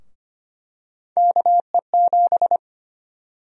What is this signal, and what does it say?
Morse code, heard as a keyed tone, about 25 wpm, 700 Hz. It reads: KE7
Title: USA Call Sign Prefixes - Repeated 2x - 25wpm